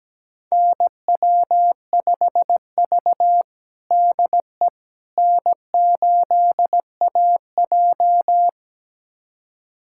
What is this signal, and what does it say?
Morse code, heard as a keyed tone, about 17 wpm, 700 Hz. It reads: NW5V DE N8AJ
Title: USA Callsign DE Callsign - 4 Characters - 17wpm